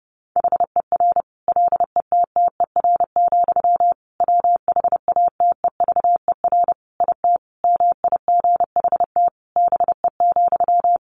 Morse code, heard as a keyed tone, about 30 wpm, 700 Hz. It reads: HER LETTER, WHATEVER IT MIGHT BE,
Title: Sentences from Top 1100 Words 30wpm